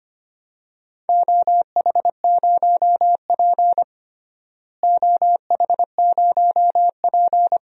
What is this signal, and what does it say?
Morse code, heard as a keyed tone, about 25 wpm, 700 Hz. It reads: OH0P OH0P
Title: DXCC-100 Call Signs - 4 Characters - Repeated 2x - 25wpm